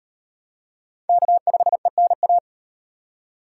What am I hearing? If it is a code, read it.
Morse code, heard as a keyed tone, about 38 wpm, 700 Hz. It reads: K5ENA